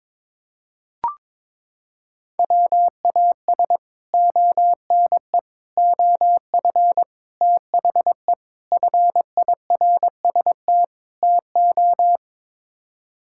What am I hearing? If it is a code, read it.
Morse code, heard as a keyed tone, about 22 wpm, 700 Hz. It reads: WAS ONE OF THE FIRST TO
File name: practice / 22wpm / Sets of 6 Words - 22wpm